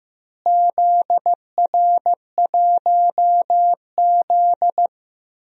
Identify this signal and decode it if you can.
Morse code, heard as a keyed tone, about 15 wpm, 700 Hz. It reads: ZR1Z
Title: DXCC-100 Call Signs - 4 Characters - Repeated 3x - 15wpm